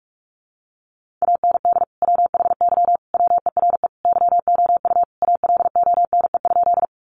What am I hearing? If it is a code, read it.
Morse code, heard as a keyed tone, about 44 wpm, 700 Hz. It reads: AND WHY WERE YOU ALONE?
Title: Sentences from Top 1000 Words 45wpm